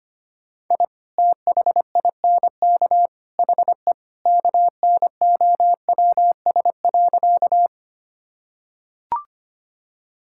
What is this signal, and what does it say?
Morse code, heard as a keyed tone, about 25 wpm, 700 Hz. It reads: I THINK HE KNOWS.
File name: practice / 25wpm / Sentences with 3-Word Phrases - 25wpm